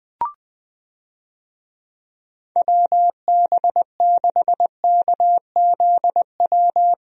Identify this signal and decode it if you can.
Morse code, heard as a keyed tone, about 20 wpm, 700 Hz. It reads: WB6KZW
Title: USA Call Signs - 6 Characters - Repeated 3x - 20wpm